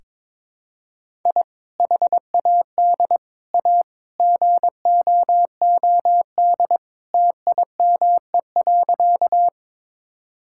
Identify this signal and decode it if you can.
Morse code, heard as a keyed tone, about 22 wpm, 700 Hz. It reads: I HAD A GOOD TIME.
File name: practice / 22wpm / Sentences with 3-Word Phrases - 22wpm